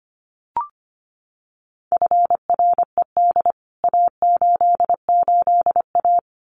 Morse code, heard as a keyed tone, about 25 wpm, 700 Hz. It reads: FRED A88A
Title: CWT Exchange 25wpm